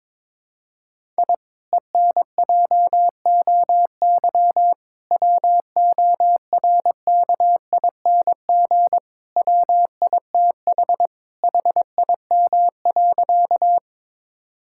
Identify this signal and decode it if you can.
Morse code, heard as a keyed tone, about 22 wpm, 700 Hz. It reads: I ENJOY WORKING WITH HIM.
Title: Sentences from Top 1000 Words - Review Part 5 - 22wpm